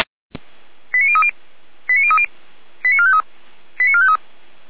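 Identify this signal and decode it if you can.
Five-tone selective call (ZVEI), heard as quick runs of five tones, tones of about 70 ms. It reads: 89030 89030 89453 89453
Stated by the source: Five-tone ZVEI selective call received on an SDR.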